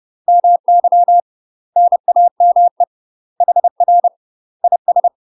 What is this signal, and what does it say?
Morse code, heard as a keyed tone, about 30 wpm, 700 Hz. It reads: MY NAME HR IS
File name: UK MASTER QSOs TRAINING / 30WPM / UK MASTER QSOs 0040